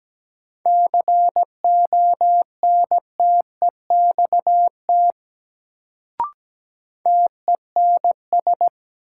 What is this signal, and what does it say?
Morse code, heard as a keyed tone, about 17 wpm, 700 Hz. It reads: CONTEXT TENS
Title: Sets of 4 Words - New Word Drill - 17wpm